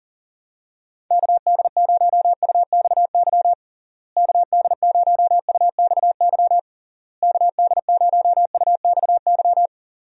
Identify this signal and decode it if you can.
Morse code, heard as a keyed tone, about 40 wpm, 700 Hz. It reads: KD0UXY KD0UXY KD0UXY
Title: USA Call Signs - 6 Characters - Repeated 3x - 40wpm